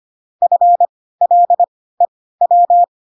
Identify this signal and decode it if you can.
Morse code, heard as a keyed tone, about 25 wpm, spaced as for 16 wpm, 700 Hz. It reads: FLEW
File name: Pinocchio / Pinocchio ch7 10 25 ik1hge